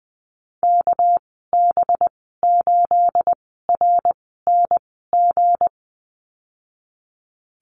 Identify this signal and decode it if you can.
Morse code, heard as a keyed tone, about 20 wpm, spaced as for 15 wpm, 700 Hz. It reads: KB8RNG
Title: 238 Lesson 38g - Letter q - Call Signs 20wpm